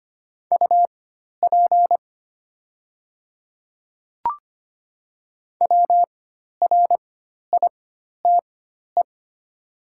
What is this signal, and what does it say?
Morse code, heard as a keyed tone, about 25 wpm, spaced as for 12 wpm, 700 Hz. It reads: UP WRITE
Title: 4x Character Spacing - Top 100 Words - 25wpm